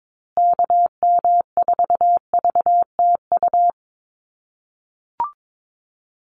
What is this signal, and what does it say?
Morse code, heard as a keyed tone, about 22 wpm, 700 Hz. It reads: KM4VTU
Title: USA Call Signs Repeated 2x 22wpm